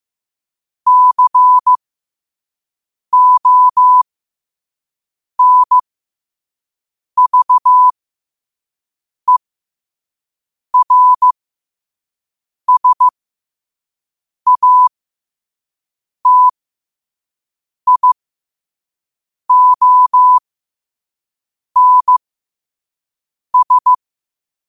Morse code, heard as a keyed tone, about 15 wpm, spaced as for 5 wpm, 1000 Hz. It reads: CONVERSATIONS
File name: qotdfiles / 5wpm